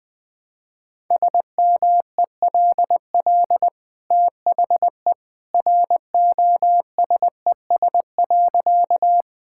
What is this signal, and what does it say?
Morse code, heard as a keyed tone, about 20 wpm, 700 Hz. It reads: SMELL THE ROSES.
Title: Easy Sentences with Hints - 20wpm